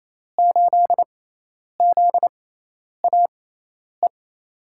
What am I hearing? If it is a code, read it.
Morse code, heard as a keyed tone, about 27 wpm, spaced as for 10 wpm, 700 Hz. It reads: 8ZAE
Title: USA Call Signs - 6 Characters - 6x Character Spacing - 28wpm